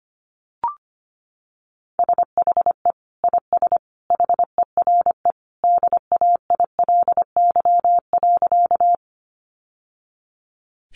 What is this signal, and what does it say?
Morse code, heard as a keyed tone, about 25 wpm, 700 Hz. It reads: SHE IS HERE DAILY.
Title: Sentences from Top 1100 Words 25wpm